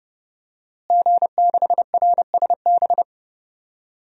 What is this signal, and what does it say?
Morse code, heard as a keyed tone, about 32 wpm, 700 Hz. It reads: G6RSB
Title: DXCC-100 Call Signs - 5 Characters - Repeated 3x - 30wpm